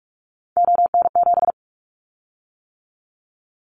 Morse code, heard as a keyed tone, about 46 wpm, 700 Hz. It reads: ON7
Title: Callsign Prefixes for Top 100 DXCC - 45wpm